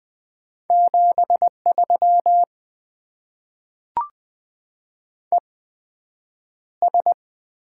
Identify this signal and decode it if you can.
Morse code, heard as a keyed tone, about 20 wpm, spaced as for 9 wpm, 700 Hz. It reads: 73 ES
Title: Top 20 QSO Elements - 8x Character Spacing - 20wpm